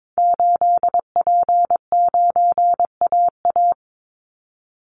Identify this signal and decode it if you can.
Morse code, heard as a keyed tone, about 22 wpm, 700 Hz. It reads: 8P9AA
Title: Call Signs - 22wpm